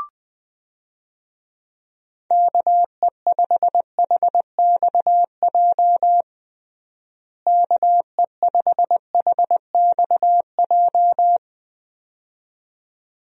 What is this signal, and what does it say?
Morse code, heard as a keyed tone, about 20 wpm, 700 Hz. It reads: KE5HXJ KE5HXJ
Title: USA Call Signs - 6 Characters - Repeated 2x - 20wpm